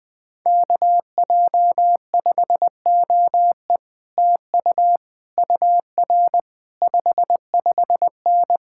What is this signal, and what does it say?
Morse code, heard as a keyed tone, about 20 wpm, 700 Hz. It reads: KJ5OE TU UR 55N